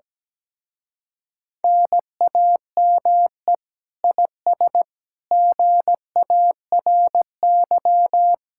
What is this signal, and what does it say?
Morse code, heard as a keyed tone, about 17 wpm, 700 Hz. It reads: NAME IS GARY
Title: Speed-Racing Common QSO Phrases - 17wpm